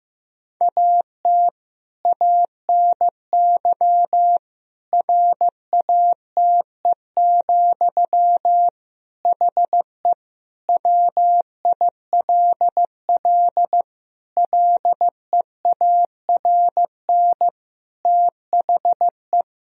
Morse code, heard as a keyed tone, about 15 wpm, 700 Hz. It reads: AT ANY RATE, HE WILL LEARN THE
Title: Sentences from Top 1000 Words - Review Part 4 - 15wpm